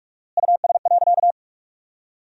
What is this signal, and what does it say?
Morse code, heard as a keyed tone, about 46 wpm, 700 Hz. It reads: US.
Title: 4x Word-Spacing - Sentences from Top 400 Words - 45wpm